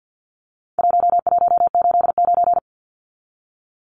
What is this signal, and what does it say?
Morse code, heard as a keyed tone, about 49 wpm, 700 Hz. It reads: 1189
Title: Numbers - 4-Digit 50wpm